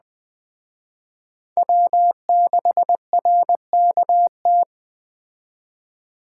Morse code, heard as a keyed tone, about 20 wpm, 700 Hz. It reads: W6RKT
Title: Speed Racing - USA Call Signs 17wpm